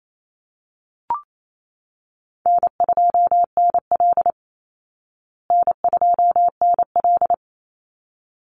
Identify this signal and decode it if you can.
Morse code, heard as a keyed tone, about 28 wpm, 700 Hz. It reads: N2NL N2NL
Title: Easy Call Signs Repeated 2x - 28wpm